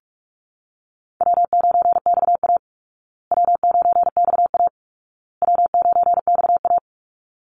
Morse code, heard as a keyed tone, about 44 wpm, 700 Hz. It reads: W9XA W9XA W9XA
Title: Call Signs Repeated 2x - 45wpm